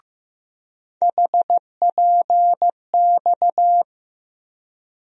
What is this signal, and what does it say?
Morse code, heard as a keyed tone, about 15 wpm, 700 Hz. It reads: HPX
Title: USA Call Sign Postfixes - 3 Letters 15wpm